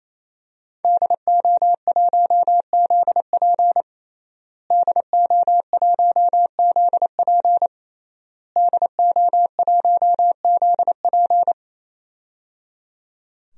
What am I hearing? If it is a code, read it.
Morse code, heard as a keyed tone, about 28 wpm, 700 Hz. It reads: DO1ZP DO1ZP DO1ZP
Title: DXCC-100 Call Signs - 5 Characters - Repeated 3x - 28wpm